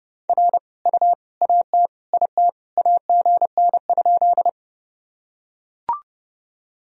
Morse code, heard as a keyed tone, about 32 wpm, 700 Hz. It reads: R U AT IT AGN?
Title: Sentences from Top 200 Words and Abbreviations 30wpm